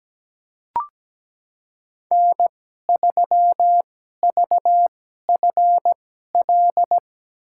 Morse code, heard as a keyed tone, about 17 wpm, spaced as for 12 wpm, 700 Hz. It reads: N3VFL
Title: USA Call Signs - 5 Characters - 2x Character Spacing - 17wpm